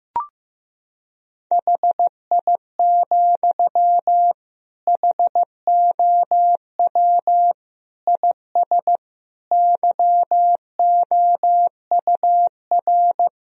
Morse code, heard as a keyed tone, about 15 wpm, 700 Hz. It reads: HI, HOW IS YOUR
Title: Sentences from Top 1000 Words - Review Part 3 - 15wpm